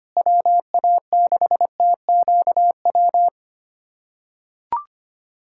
Morse code, heard as a keyed tone, about 25 wpm, 700 Hz. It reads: WA6TQW
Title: USA Callsign DE Callsign - 6 Characters - 25wpm